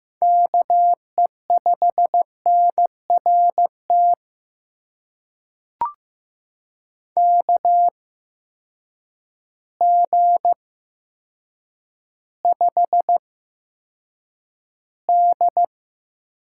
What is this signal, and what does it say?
Morse code, heard as a keyed tone, about 15 wpm, spaced as for 8 wpm, 700 Hz. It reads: KE5NRT KG5D
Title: USA Call Signs - 6 Characters - 8x Character Spacing - 15wpm